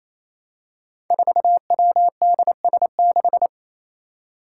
Morse code, heard as a keyed tone, about 28 wpm, 700 Hz. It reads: 4WDS6
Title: Five Challenging Letter-Number Combinations 28wpm